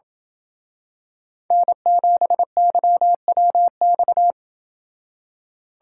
Morse code, heard as a keyed tone, about 28 wpm, 700 Hz. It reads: N7YWX